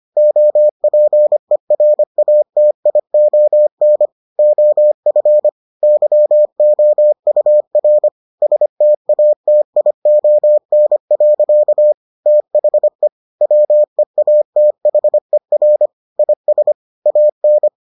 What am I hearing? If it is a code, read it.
Morse code, heard as a keyed tone, about 25 wpm, 600 Hz. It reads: OPERATION OF YOUR STATION. THE WEATHER IS AN